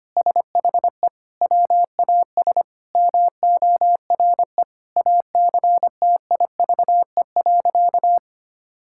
Morse code, heard as a keyed tone, about 25 wpm, 700 Hz. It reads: SHE WAS MORE ACTIVE.
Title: Sentences from Top 1000 Words - Review Part 6 - 25wpm